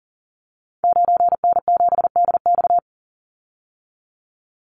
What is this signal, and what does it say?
Morse code, heard as a keyed tone, about 40 wpm, 700 Hz. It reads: 9N7DX
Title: Call Signs Repeated 2x - 40wpm